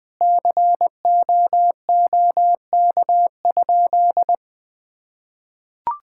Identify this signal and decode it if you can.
Morse code, heard as a keyed tone, about 21 wpm, 700 Hz. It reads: COOK?
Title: Sentences with 6-Word Phrases - Spoken Beginning - 20wpm